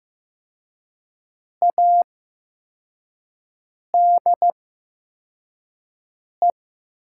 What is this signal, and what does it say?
Morse code, heard as a keyed tone, about 15 wpm, spaced as for 4 wpm, 700 Hz. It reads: ADE
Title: 8x Character Spacing - Top 200 Words - 15wpm